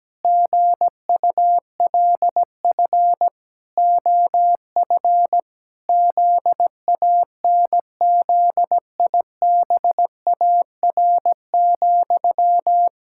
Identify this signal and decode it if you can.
Morse code, heard as a keyed tone, about 17 wpm, 700 Hz. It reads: GULF OF ZANZIBAR,
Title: Pangram Challenge v2 - 17wpm